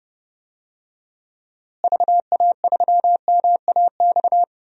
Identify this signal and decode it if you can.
Morse code, heard as a keyed tone, about 30 wpm, 700 Hz. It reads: VA3MAX
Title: DXCC-100 Call Signs - 6 Characters - Repeated 2x - 30wpm